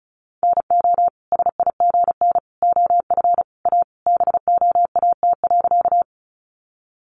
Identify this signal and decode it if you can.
Morse code, heard as a keyed tone, about 34 wpm, 700 Hz. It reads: NO SIGN OF A BOAT.
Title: Sentences from Top 1000 Words 35wpm